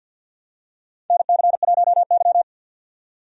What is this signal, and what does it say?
Morse code, heard as a keyed tone, about 51 wpm, 700 Hz. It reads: NC1Y
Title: USA Call Signs - 4 Characters - 50wpm